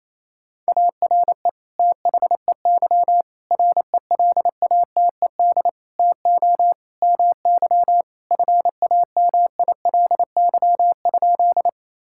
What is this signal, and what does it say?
Morse code, heard as a keyed tone, about 28 wpm, 700 Hz. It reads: ARE THEY RELATED TO MY FAMILY?
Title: Sentences from Top 1000 Words - Review Part 5 - 28wpm